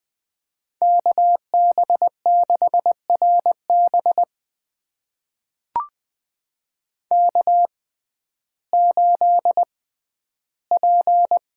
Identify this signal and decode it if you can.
Morse code, heard as a keyed tone, about 20 wpm, spaced as for 12 wpm, 700 Hz. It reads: KB6RB K8P
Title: USA Call Signs - 5 Characters - 6x Character Spacing - 20wpm